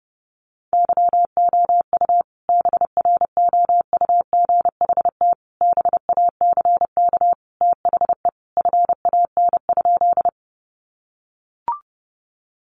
Morse code, heard as a keyed tone, about 30 wpm, 700 Hz. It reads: YOU BROUGHT BACK THE FAN?
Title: Sentences from Top 1300 Words 30wpm